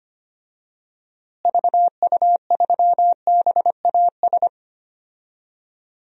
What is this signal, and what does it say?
Morse code, heard as a keyed tone, about 25 wpm, 700 Hz. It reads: VU3BAS
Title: DXCC-100 Call Signs - 6 Characters - Repeated 2x - 25wpm